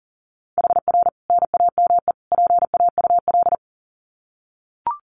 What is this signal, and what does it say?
Morse code, heard as a keyed tone, about 40 wpm, 700 Hz. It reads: HR NAME PAUL
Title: Common QSO Phrases 40wpm